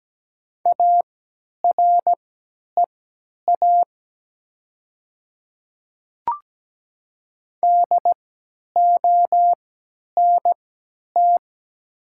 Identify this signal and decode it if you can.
Morse code, heard as a keyed tone, about 17 wpm, spaced as for 10 wpm, 700 Hz. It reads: AREA DONT
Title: 3x Character Spacing - Top 500 Words - 17wpm